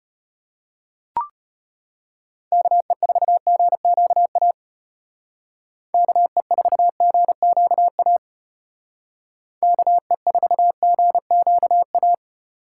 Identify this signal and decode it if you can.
Morse code, heard as a keyed tone, about 33 wpm, 700 Hz. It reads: KE4GQA KE4GQA KE4GQA